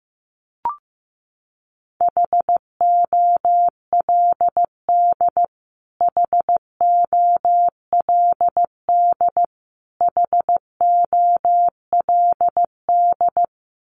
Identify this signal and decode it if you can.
Morse code, heard as a keyed tone, about 15 wpm, 700 Hz. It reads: HOLD HOLD HOLD
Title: Top 500 Words Repeated 15wpm